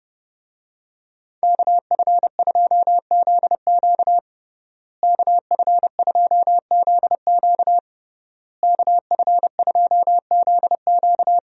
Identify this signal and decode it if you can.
Morse code, heard as a keyed tone, about 30 wpm, 700 Hz. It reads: KF2ZQ KF2ZQ KF2ZQ
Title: USA Call Signs - 5 Characters - Repeated 3x - 30wpm